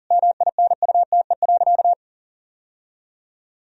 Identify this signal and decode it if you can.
Morse code, heard as a keyed tone, about 40 wpm, 700 Hz. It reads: MINUTE.
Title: Sentences from Top 1000 Words - Review Part 1 - 40wpm